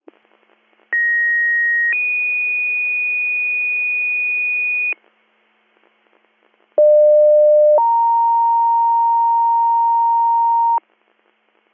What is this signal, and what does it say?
Two-tone page, heard as a one-second tone, then a three-second tone, about 1900 then 2450 Hz, about 600 then 930 Hz. It is heard from 0.9 to 10.8 s.